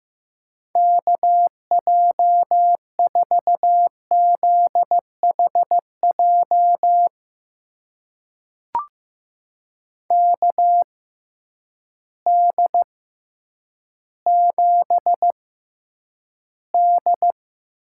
Morse code, heard as a keyed tone, about 15 wpm, spaced as for 9 wpm, 700 Hz. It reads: KJ4ZHJ KD7D
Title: USA Call Signs - 6 Characters - 6x Character Spacing - 15wpm